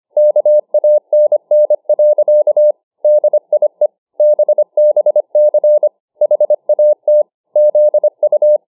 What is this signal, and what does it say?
Morse code, heard as a keyed tone, about 25 wpm, 600 Hz. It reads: KANN. DIE BBC HAT ZU